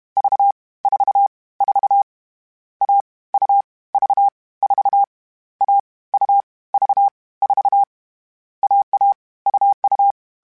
Morse code, heard as a keyed tone, about 32 wpm, 800 Hz. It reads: V 4 4 A U V 4 A U V 4 AA UU